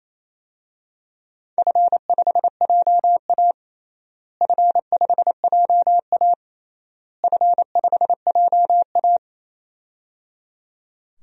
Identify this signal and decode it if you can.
Morse code, heard as a keyed tone, about 28 wpm, 700 Hz. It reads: F5JA F5JA F5JA
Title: DXCC-100 Call Signs - 4 Characters - Repeated 3x - 28wpm